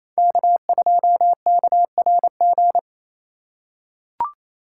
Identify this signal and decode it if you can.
Morse code, heard as a keyed tone, about 28 wpm, 700 Hz. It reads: K2KRG
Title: USA Call Signs - 5 Characters - 4x Character Spacing - 28wpm